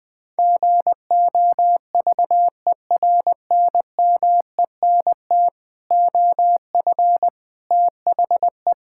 Morse code, heard as a keyed tone, about 20 wpm, 700 Hz. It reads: GOVERNMENT OF THE
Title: Sentences with 9-Word Phrases - 20wpm